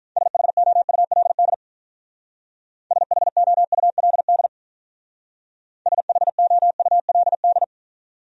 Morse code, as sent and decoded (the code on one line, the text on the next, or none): ... .... --- ..- .-.. -.. / ... .... --- ..- .-.. -.. / ... .... --- ..- .-.. -..
SHOULD SHOULD SHOULD